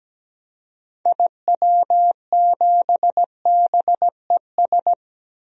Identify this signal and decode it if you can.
Morse code, heard as a keyed tone, about 17 wpm, 700 Hz. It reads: IW7BES
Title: DXCC-100 Call Signs - 6 Characters - Repeated 3x - 17wpm